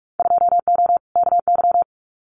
Morse code, heard as a keyed tone, about 44 wpm, 700 Hz. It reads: 2O KY
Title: Winter Field Day Exchanges 45wpm